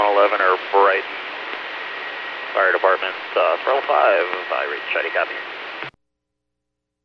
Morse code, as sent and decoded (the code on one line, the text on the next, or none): none